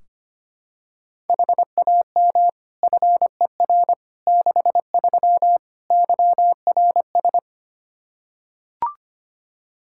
Morse code, heard as a keyed tone, about 25 wpm, 700 Hz. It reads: HAM FER 63 YRS
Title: Practice QSO - Exchange 3 - More Info - 25wpm